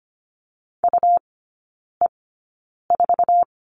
Morse code, heard as a keyed tone, about 25 wpm, spaced as for 9 wpm, 700 Hz. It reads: UE4
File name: Lesson 17e - Letter u - Three Characters 25wpm